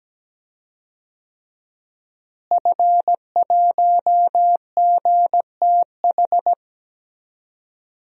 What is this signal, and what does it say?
Morse code, heard as a keyed tone, about 17 wpm, 700 Hz. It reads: F1GTH
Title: DXCC-100 Call Signs - 5 Characters - Repeated 3x - 17wpm